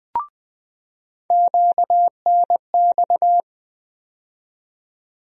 Morse code, heard as a keyed tone, about 20 wpm, 700 Hz. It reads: QNX
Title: USA Call Sign Postfixes - 3 Letters 20wpm